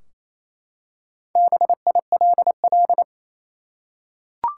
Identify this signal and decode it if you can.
Morse code, heard as a keyed tone, about 28 wpm, 700 Hz. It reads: BILL
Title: Four Letter Words - 28wpm